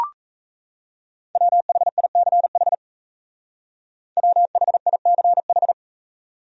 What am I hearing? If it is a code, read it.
Morse code, heard as a keyed tone, about 34 wpm, 700 Hz. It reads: WHICH WHICH
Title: Speed-Racing Top 100 Words 28wpm